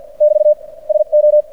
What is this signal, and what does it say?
Morse code, heard as a keyed tone, about 53 wpm, spaced as for 43 wpm, 600 Hz. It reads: <BT> SO
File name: archived / CRSnet